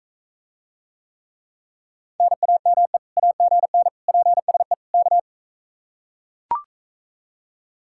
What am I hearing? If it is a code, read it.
Morse code, heard as a keyed tone, about 44 wpm, 700 Hz. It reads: NAME AGN PSE K